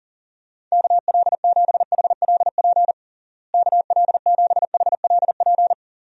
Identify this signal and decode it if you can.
Morse code, heard as a keyed tone, about 40 wpm, 700 Hz. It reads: KL7HLP KL7HLP
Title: DXCC-100 Call Signs - 6 Characters - Repeated 3x - 40wpm